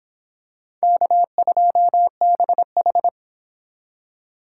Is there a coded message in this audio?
Yes: Morse code from 0.8 to 3.1 s, heard as a keyed tone, about 25 wpm, 700 Hz. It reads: K2BH